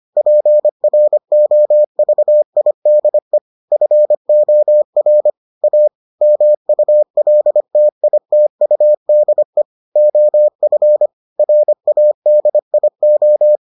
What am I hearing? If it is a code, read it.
Morse code, heard as a keyed tone, about 25 wpm, 600 Hz. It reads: PROVIDE FOR A MULTITUDE OF RADIO